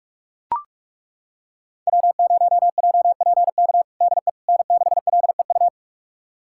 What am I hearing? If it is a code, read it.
Morse code, heard as a keyed tone, about 41 wpm, 700 Hz. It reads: W0JPK DE N6LEU
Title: USA Callsign DE Callsign - 5 Characters - 45wpm